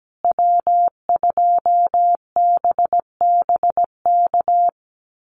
Morse code, heard as a keyed tone, about 17 wpm, 700 Hz. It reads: W2BBK
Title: USA Call Signs Repeated 2x 17wpm